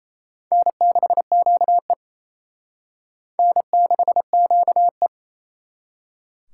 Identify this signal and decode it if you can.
Morse code, heard as a keyed tone, about 32 wpm, 700 Hz. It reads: N6QE N6QE
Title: Speed Racing - USA Call Signs 28wpm